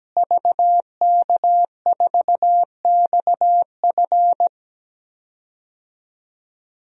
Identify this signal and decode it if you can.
Morse code, heard as a keyed tone, about 17 wpm, 700 Hz. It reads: VK4XF
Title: DXCC-100 Call Signs - 5 Characters - Repeated 2x - 17wpm